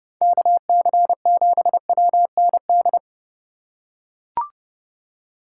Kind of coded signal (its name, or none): Morse code